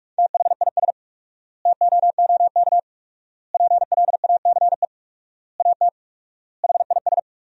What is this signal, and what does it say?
Morse code, heard as a keyed tone, about 46 wpm, 700 Hz. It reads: THIS TOOK PLACE AT HIS
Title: 4x Word-Spacing - Sentences from Top 300 Words - 45wpm